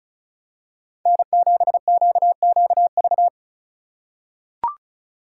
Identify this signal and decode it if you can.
Morse code, heard as a keyed tone, about 36 wpm, 700 Hz. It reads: N7QQV
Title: USA Call Signs - 5 Characters - 2x Character Spacing - 35wpm